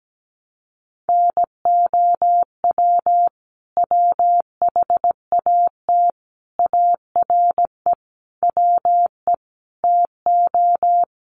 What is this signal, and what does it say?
Morse code, heard as a keyed tone, about 17 wpm, 700 Hz. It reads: NOW WHAT ARE WE TO
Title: Sentences from Top 100 Words 17wpm